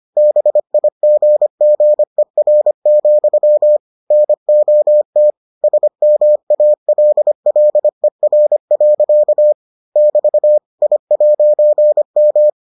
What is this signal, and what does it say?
Morse code, heard as a keyed tone, about 25 wpm, 600 Hz. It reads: BIGGER, NOT SMALLER. <BT> I'M